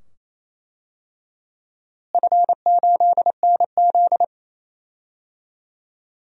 Morse code, heard as a keyed tone, about 28 wpm, 700 Hz. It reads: F8NZ
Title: DXCC-100 Call Signs - 4 Characters - Repeated 2x - 28wpm